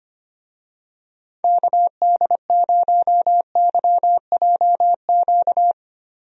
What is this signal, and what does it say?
Morse code, heard as a keyed tone, about 25 wpm, 700 Hz. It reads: KD0YJQ
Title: USA Call Signs - 6 Characters - Repeated 3x - 25wpm